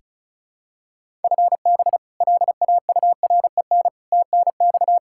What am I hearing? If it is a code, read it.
Morse code, heard as a keyed tone, about 36 wpm, 700 Hz. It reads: FB LAUREN TNX
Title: K1USN SST QSO Practice 35wpm